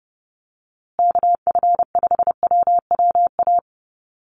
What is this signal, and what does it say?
Morse code, heard as a keyed tone, about 30 wpm, 700 Hz. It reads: KF5WWA
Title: USA Call Signs Repeated 3x 30wpm